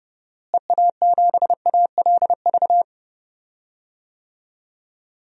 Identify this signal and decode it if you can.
Morse code, heard as a keyed tone, about 31 wpm, 700 Hz. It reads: EA7ALV
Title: DXCC-100 Call Signs - 6 Characters - Repeated 3x - 30wpm